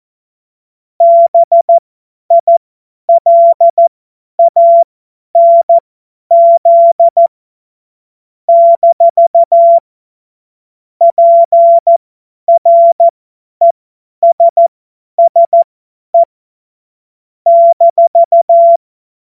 Morse code, heard as a keyed tone, about 14 wpm, spaced as for 10 wpm, 680 Hz. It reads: BILANZ - PRESSE -
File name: Morsekurs / Witze / Witz 17 Juni 2025 50 F70